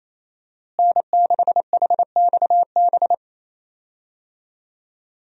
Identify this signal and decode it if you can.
Morse code, heard as a keyed tone, about 28 wpm, 700 Hz. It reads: N6HXB
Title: USA Call Signs - 5 Characters - Repeated 3x - 28wpm